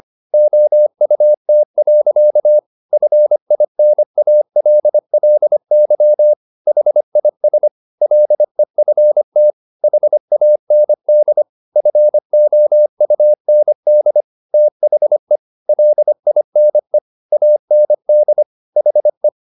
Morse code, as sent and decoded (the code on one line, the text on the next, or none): --- ..- - .-.-.- / ..-. .. -. .- .-.. .-.. -.-- / .... .. ... / .-.. ..-. - / .... .- -. -.. / ..-. --- ..- -. -.. / - .... / .-.. .. -. / .- -. -.. / .... .
OUT. FINALLY HIS LEFT HAND FOUND THE LINE AND HE